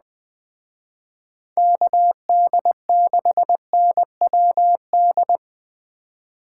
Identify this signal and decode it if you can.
Morse code, heard as a keyed tone, about 20 wpm, 700 Hz. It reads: KD6NWD